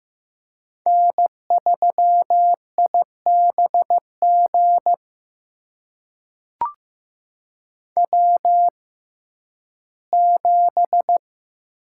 Morse code, heard as a keyed tone, about 15 wpm, spaced as for 11 wpm, 700 Hz. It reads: N3IBG W7
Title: USA Call Signs - 5 Characters - 6x Character Spacing - 15wpm